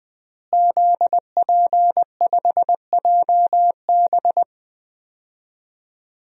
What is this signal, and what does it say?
Morse code, heard as a keyed tone, about 20 wpm, 700 Hz. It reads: ZP5JB
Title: DXCC-100 Call Signs - 5 Characters - Repeated 2x - 20wpm